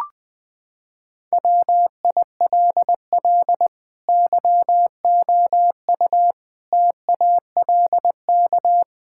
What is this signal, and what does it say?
Morse code, heard as a keyed tone, about 20 wpm, 700 Hz. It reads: WILL YOU TALK
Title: Sentences from Top 1600 Words 20wpm